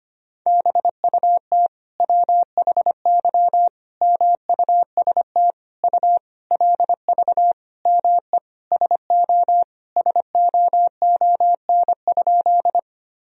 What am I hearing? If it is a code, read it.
Morse code, heard as a keyed tone, about 25 wpm, 700 Hz. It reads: BUT WHY MUST U LV ME SO SOON?
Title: Sentences from Top 400 Words and Abbreviations 25wpm